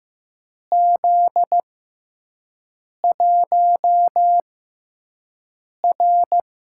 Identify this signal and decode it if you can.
Morse code, heard as a keyed tone, about 15 wpm, spaced as for 5 wpm, 700 Hz. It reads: Z1R
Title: USA Call Signs - 6x Character Spacing - 15wpm